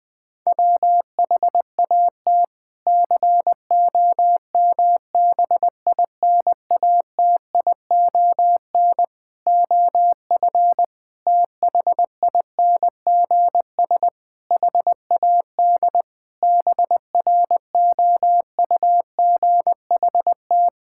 Morse code, heard as a keyed tone, about 20 wpm, 700 Hz. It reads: WHAT COMBINATION OF THINGS HAD BROUGHT
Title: Sentences from Top 1000 Words - Review Part 6 - 20wpm